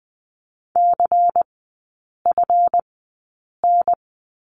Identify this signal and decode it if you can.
Morse code, heard as a keyed tone, about 20 wpm, spaced as for 8 wpm, 700 Hz. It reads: CFN
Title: 184 Lesson 30e - Letter b - Three Characters 20wpm